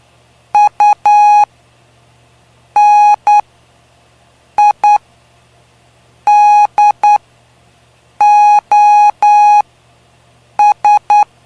Morse code, heard as a keyed tone, about 9 wpm, spaced as for 5 wpm, 830 Hz. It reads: UNIDOS
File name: cw / py2lcd / 05-PY2LCD-Li��o 5